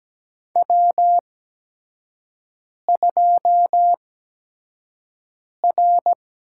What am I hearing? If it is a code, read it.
Morse code, heard as a keyed tone, about 17 wpm, spaced as for 5 wpm, 700 Hz. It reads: W2R
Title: USA Call Signs - 8x Character Spacing - 17wpm